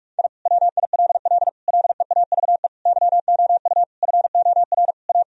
Morse code, heard as a keyed tone, about 46 wpm, 700 Hz. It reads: I WILL LEAVE YOU FOR A
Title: Sentences from Top 1000 Words - Review Part 4 - 45wpm